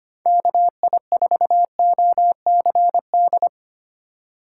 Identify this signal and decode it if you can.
Morse code, heard as a keyed tone, about 25 wpm, 700 Hz. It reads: KI4OCD
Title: USA Call Signs - 3x Character Spacing - 25wpm